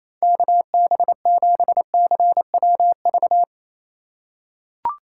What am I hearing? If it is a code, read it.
Morse code, heard as a keyed tone, about 28 wpm, 700 Hz. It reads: KB7CWV
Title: USA Call Signs - 6 Characters - 2x Character Spacing - 28wpm